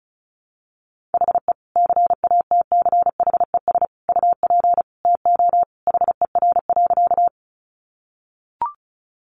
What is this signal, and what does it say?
Morse code, heard as a keyed tone, about 34 wpm, 700 Hz. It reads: HE CATCHES UP TO HER.
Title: Additional Easy Phrases 35wpm